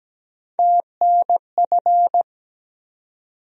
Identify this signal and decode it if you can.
Morse code, heard as a keyed tone, about 17 wpm, 700 Hz. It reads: TNF
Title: USA Call Sign Postfixes - 3 Letters 17wpm